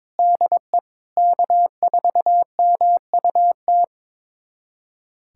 Morse code, heard as a keyed tone, about 22 wpm, 700 Hz. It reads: DE K4MUT
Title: USA Callsign DE Callsign - 5 Characters - 22wpm